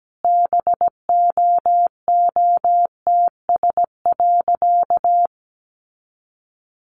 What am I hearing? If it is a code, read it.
Morse code, heard as a keyed tone, about 17 wpm, 700 Hz. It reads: BOOTS.
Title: A Sound Of Thunder 17wpm